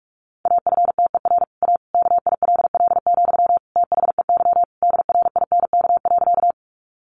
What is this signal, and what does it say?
Morse code, heard as a keyed tone, about 44 wpm, 700 Hz. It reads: AFTER A KILL, THEY DRINK.
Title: Sentences from Top 1500 Words 45wpm